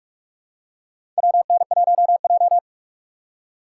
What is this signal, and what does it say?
Morse code, heard as a keyed tone, about 45 wpm, 700 Hz. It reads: WN1J
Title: USA Call Signs - 4 Characters - 3x Character Spacing - 45wpm